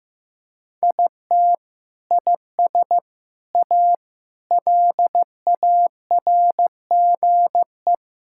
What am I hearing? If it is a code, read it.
Morse code, heard as a keyed tone, about 15 wpm, 700 Hz. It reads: IT IS A LARGE
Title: Sentences from Top 1000 Words - Review Part 5 - 15wpm